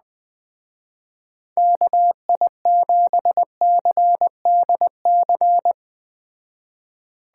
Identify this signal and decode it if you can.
Morse code, heard as a keyed tone, about 20 wpm, 700 Hz. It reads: KI7CDC